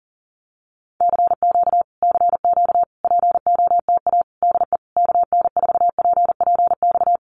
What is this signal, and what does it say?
Morse code, heard as a keyed tone, about 40 wpm, 700 Hz. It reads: CQ CQ POTA DE KN4PPX
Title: Common QSO Phrases Encore 40wpm